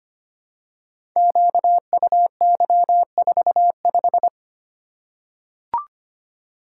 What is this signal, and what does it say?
Morse code, heard as a keyed tone, about 25 wpm, 700 Hz. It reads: QUY45
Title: Code Groups - 5 Letter:Numbers - 25wpm